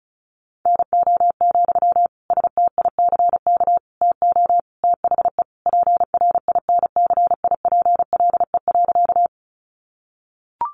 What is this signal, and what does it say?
Morse code, heard as a keyed tone, about 34 wpm, 700 Hz. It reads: NO, STICK TO THE PRINCIPLE.
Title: Sentences from Top 1400 Words 35wpm